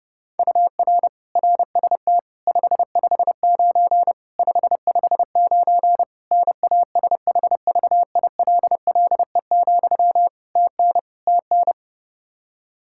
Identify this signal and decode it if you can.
Morse code, heard as a keyed tone, about 29 wpm, 700 Hz. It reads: UR RST 559 559 NASHVILLE, TN TN
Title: Practice QSO - Exchange 1 - Beginning - 30wpm